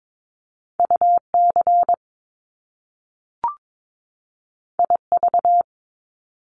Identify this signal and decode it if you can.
Morse code, heard as a keyed tone, about 22 wpm, 700 Hz. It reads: UC IV
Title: Top Two Letter Combinations in Words 22wpm